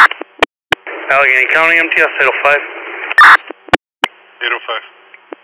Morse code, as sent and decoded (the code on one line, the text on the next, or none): none